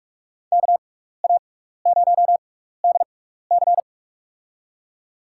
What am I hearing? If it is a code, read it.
Morse code, heard as a keyed tone, about 45 wpm, spaced as for 15 wpm, 700 Hz. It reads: KA0DC
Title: USA Call Signs - 5 Characters - 6x Character Spacing - 45wpm